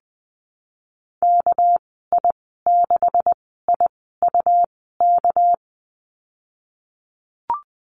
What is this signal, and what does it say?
Morse code, heard as a keyed tone, about 20 wpm, 700 Hz. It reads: K I 6 I U K
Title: 200 Lesson 32g - Letter k - Call Signs 20wpm